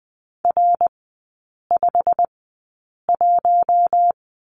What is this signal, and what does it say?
Morse code, heard as a keyed tone, about 20 wpm, spaced as for 8 wpm, 690 Hz. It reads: R51
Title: 087 Lesson 15e - Number 5 - Three Characters 20wpm